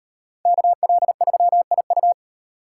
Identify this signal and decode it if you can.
Morse code, heard as a keyed tone, about 38 wpm, 700 Hz. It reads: KL3IU